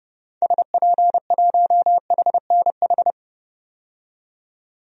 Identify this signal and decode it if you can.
Morse code, heard as a keyed tone, about 32 wpm, 700 Hz. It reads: SP1HNH
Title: DXCC-100 Call Signs - 6 Characters - 30wpm